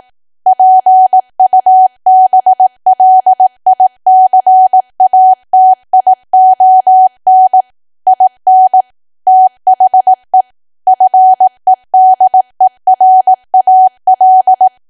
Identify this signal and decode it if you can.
Morse code, heard as a keyed tone, about 18 wpm, 750 Hz. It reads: PUBLICATION IN THE FEDERAL
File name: cwa / W1AW-Level2 / 18WPM